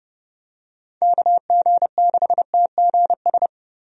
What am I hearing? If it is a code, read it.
Morse code, heard as a keyed tone, about 30 wpm, 700 Hz. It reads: KG6TGS
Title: USA Call Signs - 6 Characters - Repeated 2x - 30wpm